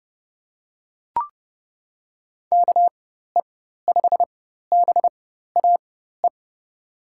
Morse code, heard as a keyed tone, about 30 wpm, spaced as for 14 wpm, 700 Hz. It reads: KE5BAE
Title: USA Call Signs - 6 Characters - 4x Character Spacing - 30wpm